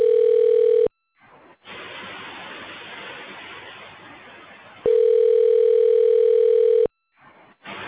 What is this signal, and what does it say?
Ringback tone (North America), 440 + 480 Hz.